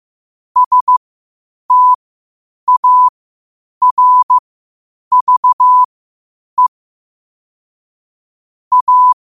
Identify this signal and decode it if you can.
Morse code, heard as a keyed tone, about 15 wpm, spaced as for 8 wpm, 1000 Hz. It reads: STARVE A
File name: qotdfiles / 8wpm